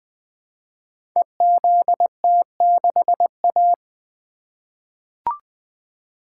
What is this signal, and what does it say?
Morse code, heard as a keyed tone, about 20 wpm, 700 Hz. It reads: EZT6A
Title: Code Groups - 5 Letter:Numbers - 20wpm